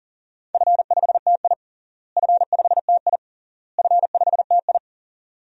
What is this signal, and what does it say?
Morse code, heard as a keyed tone, about 40 wpm, 700 Hz. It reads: F5TI F5TI F5TI
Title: DXCC-100 Call Signs - 4 Characters - Repeated 3x - 40wpm